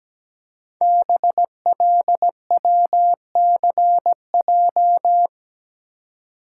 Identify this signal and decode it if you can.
Morse code, heard as a keyed tone, about 17 wpm, 700 Hz. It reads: BLWCJ
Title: Code Groups - 5 Letters - 17wpm